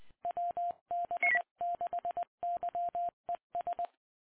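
Morse code, heard as a keyed tone, about 20 wpm, 690 Hz. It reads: WB6YES
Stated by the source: Morse ident of the PAPA System analog repeaters, WD6FZA.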